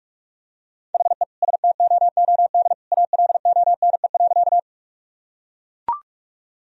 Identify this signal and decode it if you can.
Morse code, heard as a keyed tone, about 44 wpm, 700 Hz. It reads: HE STOOD ALONE.
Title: Sentences from Top 1000 Words - Review Part 2 - 45wpm